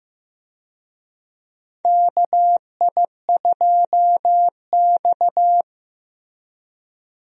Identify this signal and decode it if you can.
Morse code, heard as a keyed tone, about 15 wpm, 700 Hz. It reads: KI2X
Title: USA Call Signs - 4 Characters - Repeated 3x - 15wpm